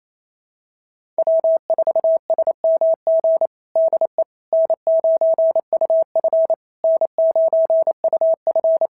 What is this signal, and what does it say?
Morse code, heard as a keyed tone, about 25 wpm, 650 Hz. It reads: W4SMG DE N9UF N9UF
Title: SOTA QSO Practice 28wpm